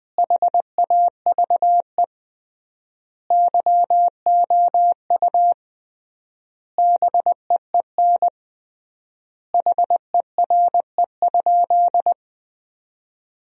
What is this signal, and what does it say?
Morse code, heard as a keyed tone, about 20 wpm, 700 Hz. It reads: HAVE YOU BEEN HERE?
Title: 3x Word-Spacing - Sentences from Top 200 Words - 20wpm